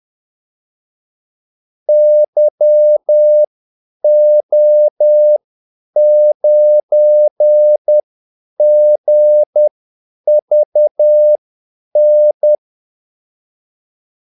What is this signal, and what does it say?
Morse code, heard as a keyed tone, about 10 wpm, spaced as for 8 wpm, 600 Hz. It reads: YO9GVN
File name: Callsigns-10-8wpm